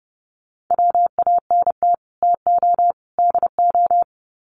Morse code, heard as a keyed tone, about 30 wpm, 700 Hz. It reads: WANT TO DO
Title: Sets of 3 Words 30wpm